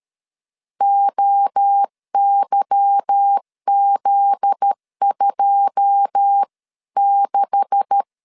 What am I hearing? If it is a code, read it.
Morse code, heard as a keyed tone, about 13 wpm, 800 Hz. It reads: OYZ26
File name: LEZIONI / 9 2867 12wpm 2322